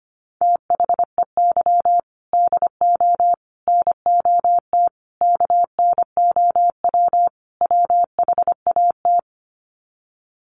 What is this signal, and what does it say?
Morse code, heard as a keyed tone, about 25 wpm, 700 Hz. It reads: THEY DO NOT KNOW WHAT
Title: Sets of 5 Words Encore 25wpm